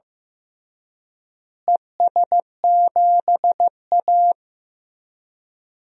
Morse code, heard as a keyed tone, about 15 wpm, 700 Hz. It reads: ES7A
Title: DXCC-100 Call Signs - 4 Characters - Repeated 3x - 15wpm